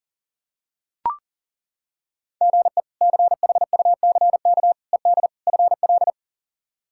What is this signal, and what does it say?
Morse code, heard as a keyed tone, about 40 wpm, 700 Hz. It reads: GE CHUCK ED FL
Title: K1USN SST QSO Practice - 40wpm